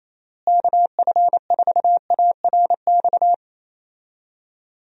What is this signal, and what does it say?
Morse code, heard as a keyed tone, about 28 wpm, 700 Hz. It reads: KF4ARX
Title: USA Call Signs - 6 Characters - Repeated 2x - 28wpm